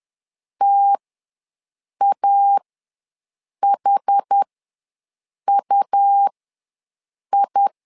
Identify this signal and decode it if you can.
Morse code, heard as a keyed tone, about 10 wpm, spaced as for 6 wpm, 800 Hz. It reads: TAHUI
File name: LEZIONI / 2 UHDN / 2 UH ws 10wpm 2029